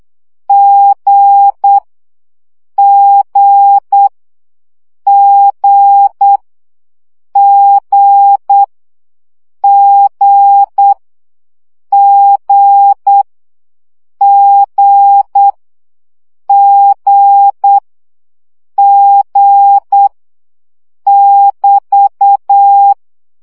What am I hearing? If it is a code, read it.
Morse code, heard as a keyed tone, about 8 wpm, 800 Hz. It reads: G G G G G G G G G <BT>
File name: CW gyakorlas g 7 baud